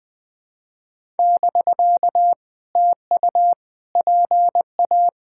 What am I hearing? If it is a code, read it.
Morse code, heard as a keyed tone, about 20 wpm, 700 Hz. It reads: <BK> TU PA
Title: POTA QSO Practice - 20wpm